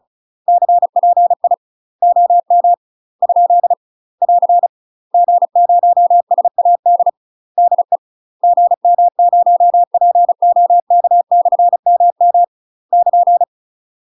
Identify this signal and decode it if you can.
Morse code, heard as a keyed tone, about 36 wpm, 700 Hz. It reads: CPI OM ? <AR> G0SAD DE GM0POK/MM <KN>
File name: UK MASTER QSOs TRAINING / 35WPM / UK MASTER QSOs 0007